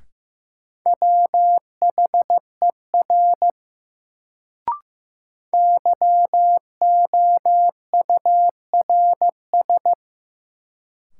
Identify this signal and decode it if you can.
Morse code, heard as a keyed tone, about 15 wpm, 700 Hz. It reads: WHER YOURS
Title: Sentences from Top 1500 Words - New Word Drill - 15wpm